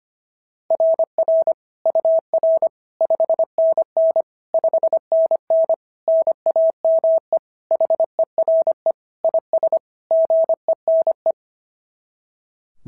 Morse code, heard as a keyed tone, about 25 wpm, 690 Hz. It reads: RR UR 5NN 5NN NAME HERE IS GENE
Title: SOTA QSO Practice 25wpm